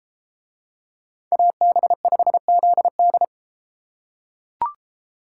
Morse code, heard as a keyed tone, about 34 wpm, 700 Hz. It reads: AB5ZD